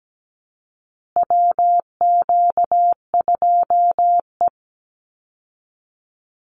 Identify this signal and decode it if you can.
Morse code, heard as a keyed tone, about 17 wpm, 700 Hz. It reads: WQ2E
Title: Call Signs Repeated 2x - 17wpm